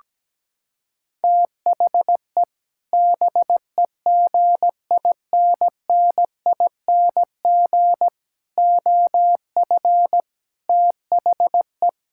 Morse code, heard as a keyed tone, about 17 wpm, 700 Hz. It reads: THE BEGINNING OF THE
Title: Sentences from Top 1000 Words - Review Part 6 - 17wpm